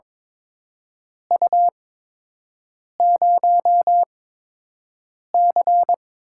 Morse code, heard as a keyed tone, about 22 wpm, spaced as for 6 wpm, 700 Hz. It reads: U0C
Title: USA Call Signs - 4 Characters - 8x Character Spacing - 22wpm